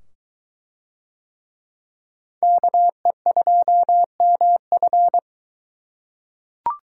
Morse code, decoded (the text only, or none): KE2MF